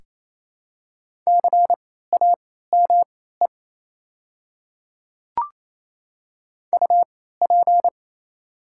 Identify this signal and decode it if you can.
Morse code, heard as a keyed tone, about 28 wpm, spaced as for 16 wpm, 700 Hz. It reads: CAME UP
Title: 3x Character Spacing - Top 500 Words - 28wpm